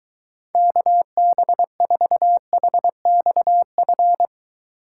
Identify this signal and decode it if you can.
Morse code, heard as a keyed tone, about 23 wpm, 700 Hz. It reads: KB4HXF